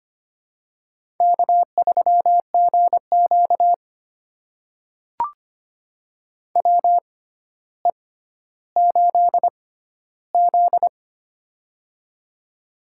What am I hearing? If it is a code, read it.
Morse code, heard as a keyed tone, about 25 wpm, spaced as for 13 wpm, 700 Hz. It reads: K3GQ WE8Z
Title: USA Call Signs - 6x Character Spacing - 25wpm